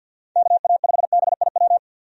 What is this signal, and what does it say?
Morse code, heard as a keyed tone, about 50 wpm, 700 Hz. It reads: KA5BIW
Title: USA Call Signs - 6 Characters - 50wpm